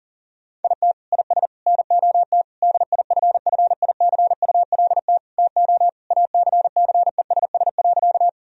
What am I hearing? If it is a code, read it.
Morse code, heard as a keyed tone, about 37 wpm, 700 Hz. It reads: IT IS NOT DIFFICULT TO ACCESS.
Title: Sentences from Top 1000 Words - Review Part 4 - 40wpm